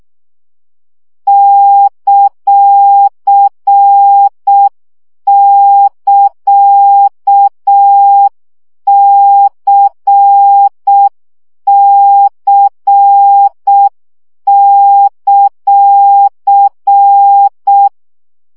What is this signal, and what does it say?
Morse code, heard as a keyed tone, about 6 wpm, 800 Hz. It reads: ;<KA>CC;
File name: CW gyakorlas c ct pontosvesszo 5 baud